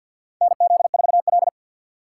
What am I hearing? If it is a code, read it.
Morse code, heard as a keyed tone, about 51 wpm, 700 Hz. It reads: NZ4L
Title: USA Call Signs - 2x Character Spacing - 50wpm